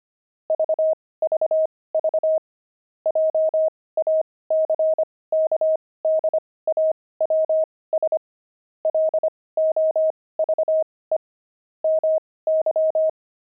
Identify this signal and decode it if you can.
Morse code, heard as a keyed tone, about 25 wpm, spaced as for 18 wpm, 630 Hz. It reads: VVV JACKDAWS LOVE MY
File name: pangram-fs-18-cs-25-jackdaws